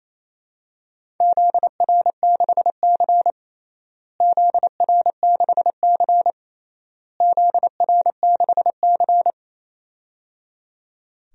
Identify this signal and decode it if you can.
Morse code, heard as a keyed tone, about 28 wpm, 700 Hz. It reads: ZR6C ZR6C ZR6C
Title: DXCC-100 Call Signs - 4 Characters - Repeated 3x - 28wpm